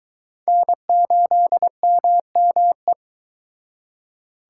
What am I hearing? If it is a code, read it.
Morse code, heard as a keyed tone, about 24 wpm, 700 Hz. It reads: N8MME